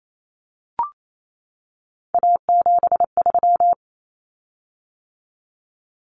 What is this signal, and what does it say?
Morse code, heard as a keyed tone, about 28 wpm, 700 Hz. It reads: A73
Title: Common Cut Numbers - 3-Digit 28wpm